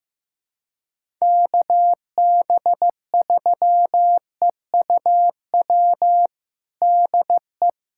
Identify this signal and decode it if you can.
Morse code, heard as a keyed tone, about 16 wpm, 700 Hz. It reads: KB3EUW DE
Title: USA Callsign DE Callsign - 6 Characters - 15wpm